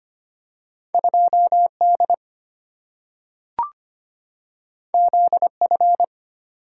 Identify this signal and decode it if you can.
Morse code, heard as a keyed tone, about 25 wpm, 690 Hz. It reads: 2D ZF
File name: Two Letter-Number Combinations - 25wpm